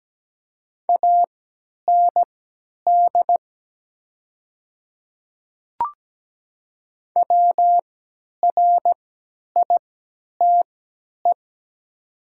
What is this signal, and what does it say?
Morse code, heard as a keyed tone, about 17 wpm, spaced as for 10 wpm, 690 Hz. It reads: AND WRITE
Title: 3x Character Spacing - Top 100 Words - 17wpm